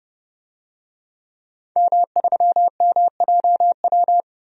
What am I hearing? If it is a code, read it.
Morse code, heard as a keyed tone, about 30 wpm, 700 Hz. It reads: M3MJW
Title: DXCC-100 Call Signs - 5 Characters - 30wpm